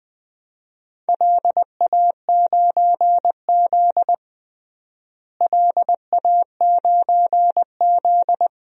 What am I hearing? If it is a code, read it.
Morse code, heard as a keyed tone, about 20 wpm, 700 Hz. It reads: LA9Z LA9Z
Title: DXCC-100 Call Signs - 4 Characters - Repeated 2x - 20wpm